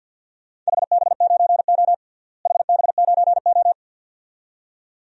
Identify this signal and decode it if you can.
Morse code, heard as a keyed tone, about 52 wpm, 700 Hz. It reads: HB9O HB9O
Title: DXCC-100 Call Signs - 4 Characters - Repeated 2x - 50wpm